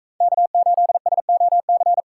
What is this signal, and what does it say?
Morse code, heard as a keyed tone, about 44 wpm, 700 Hz. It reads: K8SOC